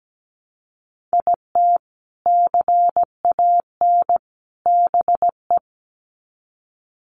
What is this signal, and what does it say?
Morse code, heard as a keyed tone, about 17 wpm, 700 Hz. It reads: IT CAN BE
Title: Sets of 3 Words 17wpm